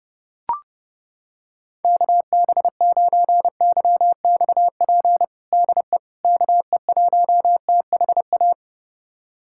Morse code, heard as a keyed tone, about 30 wpm, 700 Hz. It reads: KB9YXP DE KE1THA
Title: USA Callsign DE Callsign 30wpm